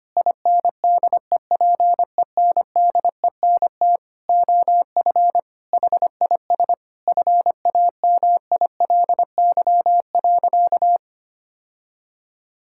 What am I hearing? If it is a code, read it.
Morse code, heard as a keyed tone, about 25 wpm, 700 Hz. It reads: INDEPENDENT OF HIS FAMILY.
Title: Sentences from Top 1000 Words - Review Part 6 - 25wpm